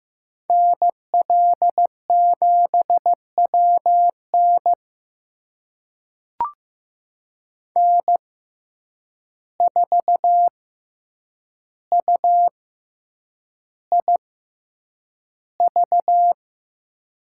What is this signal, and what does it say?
Morse code, heard as a keyed tone, about 15 wpm, spaced as for 8 wpm, 700 Hz. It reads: NL7WN N4UIV
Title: USA Call Signs - 5 Characters - 6x Character Spacing - 15wpm